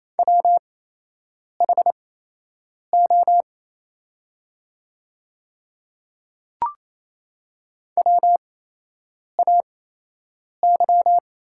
Morse code, heard as a keyed tone, about 28 wpm, spaced as for 8 wpm, 700 Hz. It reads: WHO WAY
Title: 8x Character Spacing - Top 100 Words - 28wpm